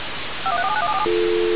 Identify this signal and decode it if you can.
DTMF, heard as telephone touch tones, tones of about 70 ms. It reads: 5134791747
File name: radiocap / REC z